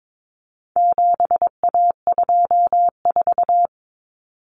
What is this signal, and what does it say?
Morse code, heard as a keyed tone, about 22 wpm, 700 Hz. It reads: 7A24
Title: Common Cut Numbers - 4-Digit 22wpm